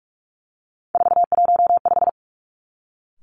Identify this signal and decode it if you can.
Morse code, heard as a keyed tone, about 46 wpm, 700 Hz. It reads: 415
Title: Three Challenging Letter-Number Combinations 45wpm